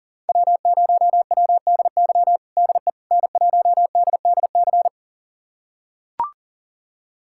Morse code, heard as a keyed tone, about 37 wpm, 700 Hz. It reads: W0WDY DE N1DDC
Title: USA Callsign DE Callsign - 5 Characters - 40wpm